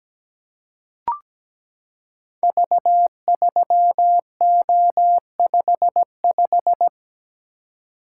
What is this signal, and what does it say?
Morse code, heard as a keyed tone, about 17 wpm, 700 Hz. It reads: V3O55
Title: Code Groups - 5 Letter:Numbers - 17wpm